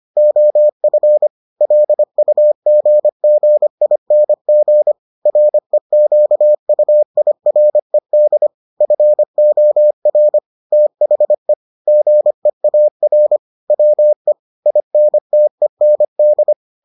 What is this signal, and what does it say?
Morse code, heard as a keyed tone, about 25 wpm, 600 Hz. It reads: OF LUGGING REQUIRED FOR THE GEAR WE INTEND